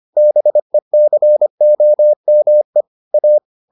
Morse code, heard as a keyed tone, about 25 wpm, 600 Hz. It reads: BECOME A